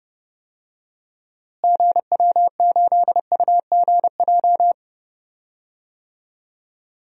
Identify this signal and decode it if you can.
Morse code, heard as a keyed tone, about 30 wpm, 700 Hz. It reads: GW8UGJ
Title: DXCC-100 Call Signs - 6 Characters - Repeated 3x - 30wpm